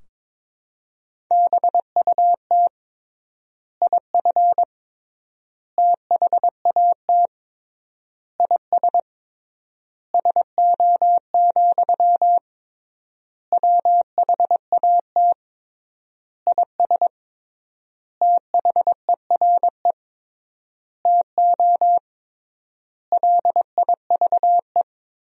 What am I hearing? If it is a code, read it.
Morse code, heard as a keyed tone, about 22 wpm, 700 Hz. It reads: BUT IF THAT IS SO, WHAT IS THERE TO LIVE
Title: 3x Word-Spacing - Sentences from Top 200 Words - 22wpm